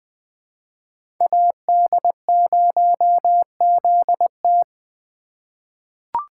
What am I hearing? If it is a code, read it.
Morse code, heard as a keyed tone, about 20 wpm, 700 Hz. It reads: AD0ZT
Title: USA Call Signs - 5 Characters - 3x Character Spacing - 20wpm